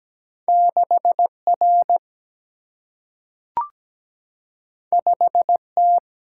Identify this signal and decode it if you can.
Morse code, heard as a keyed tone, about 17 wpm, 700 Hz. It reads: 6 R 5 T
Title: Single-Letter Single-Number 17wpm